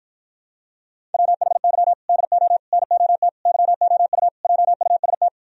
Morse code, heard as a keyed tone, about 47 wpm, 700 Hz. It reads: WHY DO NOT YOU JUST